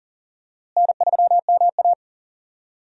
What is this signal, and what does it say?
Morse code, heard as a keyed tone, about 40 wpm, 700 Hz. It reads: N3MA
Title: USA Call Signs - 4 Characters - Repeated 3x - 40wpm